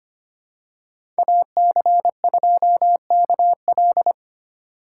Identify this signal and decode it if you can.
Morse code, heard as a keyed tone, about 25 wpm, 700 Hz. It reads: AC2KL
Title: USA Call Signs - 3x Character Spacing - 25wpm